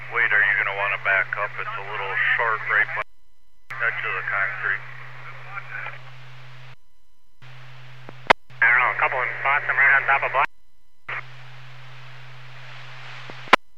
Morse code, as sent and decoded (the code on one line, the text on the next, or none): none